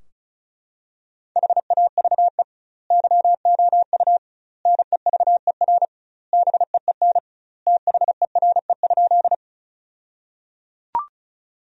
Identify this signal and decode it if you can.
Morse code, heard as a keyed tone, about 36 wpm, 700 Hz. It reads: HAVE YOU NEVER BEEN THERE?
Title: 2x Word-Spacing - Sentences from Top 300 Words - 35wpm